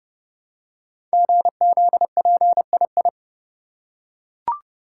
Morse code, heard as a keyed tone, about 30 wpm, 700 Hz. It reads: GZPII
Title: Code Groups - 5 Letter:Numbers - 30wpm